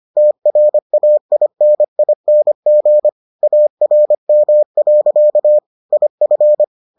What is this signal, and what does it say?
Morse code, heard as a keyed tone, about 25 wpm, 600 Hz. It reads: TRAINING ARM. IF